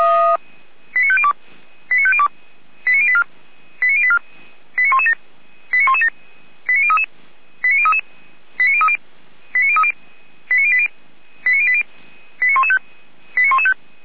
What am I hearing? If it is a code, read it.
Five-tone selective call (ZVEI), heard as quick runs of five tones, tones of about 80 ms. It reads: 89502 89502 89084 89084 89117 89117 89033 89033 89030 89080 89080 89115